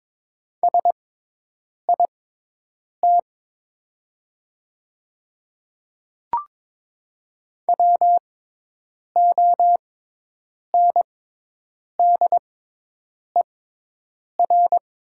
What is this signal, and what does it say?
Morse code, heard as a keyed tone, about 22 wpm, spaced as for 8 wpm, 700 Hz. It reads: SIT WONDER
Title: 6x Character Spacing - Top 500 Words - 22wpm